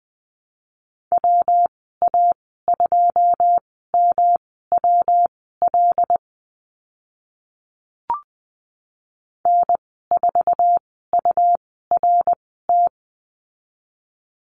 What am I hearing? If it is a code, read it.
Morse code, heard as a keyed tone, about 20 wpm, spaced as for 14 wpm, 700 Hz. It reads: WA2MWL N4URT
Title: 140 Lesson 23g - Letter f - Call Signs 20wpm